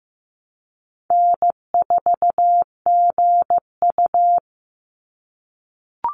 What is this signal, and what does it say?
Morse code, heard as a keyed tone, about 15 wpm, 700 Hz. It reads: N4GU
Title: Easy Call Signs - 15wpm